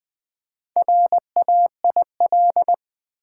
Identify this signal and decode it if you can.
Morse code, heard as a keyed tone, about 20 wpm, 700 Hz. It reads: RAIL
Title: Build A Word - 2 Letters At A Time - 4-Letter Words - 20wpm